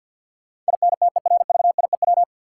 Morse code, heard as a keyed tone, about 48 wpm, 700 Hz. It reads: INTERVIEW
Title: Sentences from Top 1600 Words - New Word Drill - 50wpm